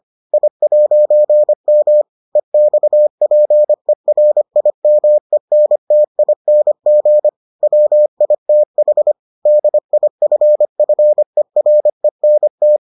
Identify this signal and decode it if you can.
Morse code, heard as a keyed tone, about 25 wpm, 600 Hz. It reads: I'M EXPERIMENTING WITH DIFFERENT